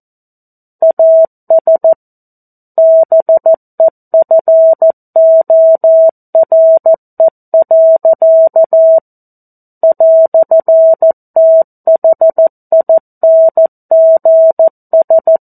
Morse code, heard as a keyed tone, about 14 wpm, 650 Hz. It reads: AS BEFORE. "THINGS
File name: 02-Chapter-02 14 WPM